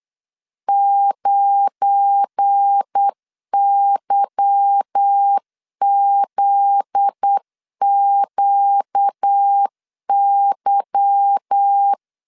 Morse code, heard as a keyed tone, about 8 wpm, 800 Hz. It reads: 9YZQY
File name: LEZIONI / 8 YJ45 / 8 Y 8wpm 3245